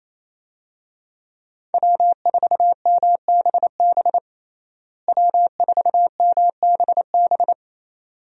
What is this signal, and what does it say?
Morse code, heard as a keyed tone, about 28 wpm, 700 Hz. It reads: W4MBB W4MBB
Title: USA Call Signs - 5 Characters - Repeated 2x - 28wpm